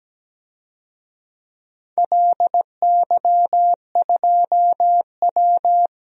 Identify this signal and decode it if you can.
Morse code, heard as a keyed tone, about 17 wpm, 700 Hz. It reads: LY2W
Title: DXCC-100 Call Signs - 4 Characters - Repeated 2x - 17wpm